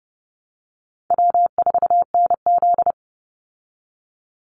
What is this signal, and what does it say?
Morse code, heard as a keyed tone, about 30 wpm, 700 Hz. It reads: W4NZ
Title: Call Signs Repeated 3x - 30wpm